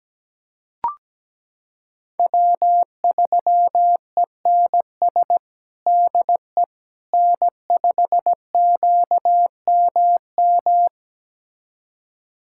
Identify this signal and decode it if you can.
Morse code, heard as a keyed tone, about 17 wpm, 700 Hz. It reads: W3ENS DE N5QMM
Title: USA Callsign DE Callsign - 5 Characters - 17wpm